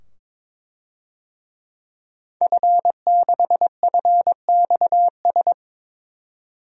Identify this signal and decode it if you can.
Morse code, heard as a keyed tone, about 22 wpm, 700 Hz. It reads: F6FXS
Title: DXCC-100 Call Signs - 5 Characters - 22wpm